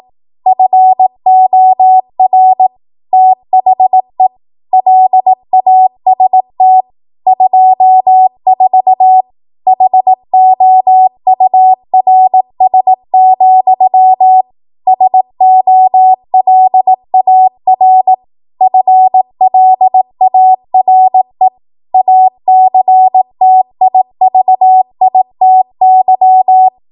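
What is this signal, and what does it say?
Morse code, heard as a keyed tone, about 18 wpm, 750 Hz. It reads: FOR THE LAST 24 HOURS, SOLAR FLARE ACTIVITY
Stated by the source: Code practice at 18 WPM, a transition run bridging 15 to 20 WPM.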